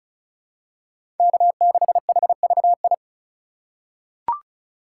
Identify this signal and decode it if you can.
Morse code, heard as a keyed tone, about 34 wpm, 690 Hz. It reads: K6HVI
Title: USA Call Signs - 5 Characters - 3x Character Spacing - 35wpm